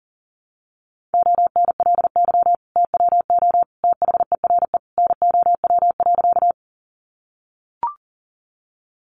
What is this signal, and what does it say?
Morse code, heard as a keyed tone, about 40 wpm, 700 Hz. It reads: ONLY TWO THERE NOW.
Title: Sentences from Top 200 Words 40wpm